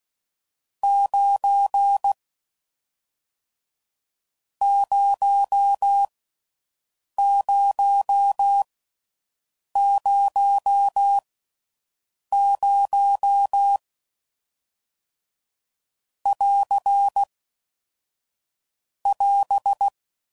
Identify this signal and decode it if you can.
Morse code, heard as a keyed tone, about 16 wpm, 800 Hz. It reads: 9 0 0 0 0 <AR> <AS>